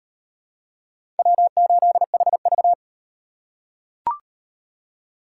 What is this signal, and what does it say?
Morse code, heard as a keyed tone, about 38 wpm, 700 Hz. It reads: W8HV